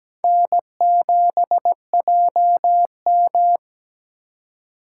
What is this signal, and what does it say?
Morse code, heard as a keyed tone, about 20 wpm, 700 Hz. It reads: N7JM
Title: Speed Racing - USA 1x2 Call Signs 17wpm